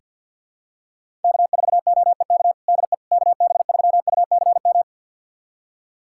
Morse code, heard as a keyed tone, about 49 wpm, 700 Hz. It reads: K4OEK DE KB3UCM
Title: USA Callsign DE Callsign 50wpm